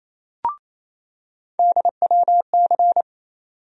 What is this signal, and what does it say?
Morse code, heard as a keyed tone, about 28 wpm, 700 Hz. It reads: DWC
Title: USA Call Sign Postfixes - 3 Letters 28wpm